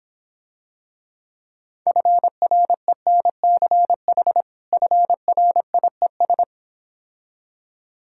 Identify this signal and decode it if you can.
Morse code, heard as a keyed tone, about 26 wpm, 700 Hz. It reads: FRENCH FRIES